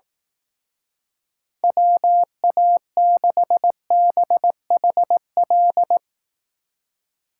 Morse code, heard as a keyed tone, about 18 wpm, 700 Hz. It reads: WA6BHL
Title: Speed Racing - USA 2x3 Call Signs 15wpm